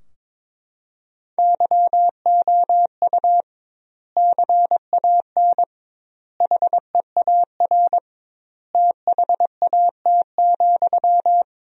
Morse code, heard as a keyed tone, about 21 wpm, 700 Hz. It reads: YOU CAN HEAR THAT,
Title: 2x Word-Spacing - Sentences from Top 500 Words - 22wpm